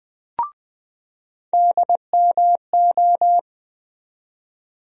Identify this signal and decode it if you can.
Morse code, heard as a keyed tone, about 20 wpm, 700 Hz. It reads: DMO
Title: USA Call Sign Postfixes - 3 Letters 20wpm